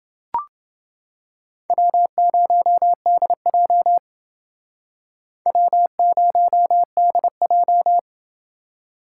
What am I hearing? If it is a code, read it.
Morse code, heard as a keyed tone, about 28 wpm, 700 Hz. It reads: W0DJ W0DJ